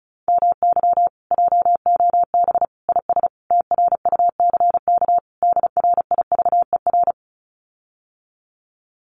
Morse code, heard as a keyed tone, about 34 wpm, 700 Hz. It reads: MY JOB IS TRUCK DRIVER
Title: Common QSO Phrases Encore 35wpm